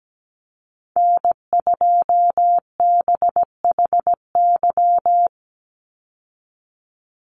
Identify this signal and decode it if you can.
Morse code, heard as a keyed tone, about 17 wpm, 700 Hz. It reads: N2BHY
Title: USA Call Signs 17wpm